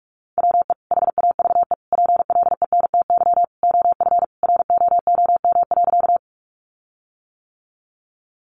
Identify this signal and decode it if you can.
Morse code, heard as a keyed tone, about 44 wpm, 700 Hz. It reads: WE HAVE PLENTY OF ROOM.
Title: Sentences from Top 1300 Words 45wpm